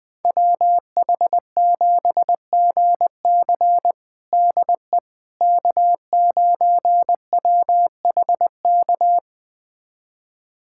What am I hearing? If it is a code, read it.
Morse code, heard as a keyed tone, about 20 wpm, 700 Hz. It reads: WH7GC DE K9WHK
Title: USA Callsign DE Callsign - 5 Characters - 20wpm